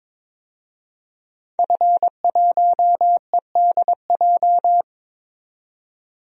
Morse code, heard as a keyed tone, about 22 wpm, 700 Hz. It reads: F1EDJ
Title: DXCC-100 Call Signs - 5 Characters - 22wpm